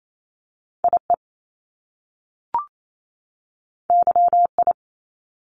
Morse code, heard as a keyed tone, about 28 wpm, 700 Hz. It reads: IE YI
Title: Top Two Letter Combinations in Words 28wpm